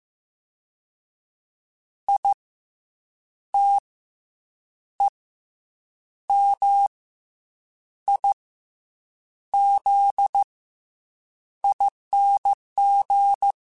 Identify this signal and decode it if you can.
Morse code, heard as a keyed tone, about 15 wpm, spaced as for 7 wpm, 800 Hz. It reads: ITEMIZING
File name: Special INGs-mp3s / Special INGs-mp3s / Fast ing @13